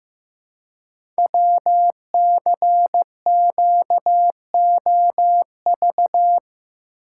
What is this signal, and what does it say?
Morse code, heard as a keyed tone, about 15 wpm, 700 Hz. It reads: WCQOV
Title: Code Groups - 5 Letters - 15wpm